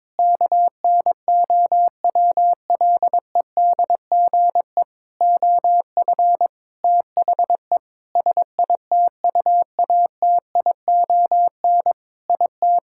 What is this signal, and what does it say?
Morse code, heard as a keyed tone, about 22 wpm, 700 Hz. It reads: KNOWLEDGE OF THE SITUATION IT
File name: Sentences with 9-Word Phrases - 22wpm